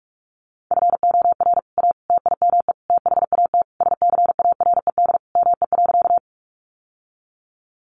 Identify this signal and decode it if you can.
Morse code, heard as a keyed tone, about 44 wpm, 700 Hz. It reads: FOR A TIME THAT SCARED ME.
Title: Sentences from Top 700 Words 45wpm